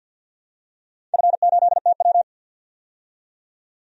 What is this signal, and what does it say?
Morse code, heard as a keyed tone, about 51 wpm, 700 Hz. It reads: F8TW
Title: DXCC-100 Call Signs - 4 Characters - 50wpm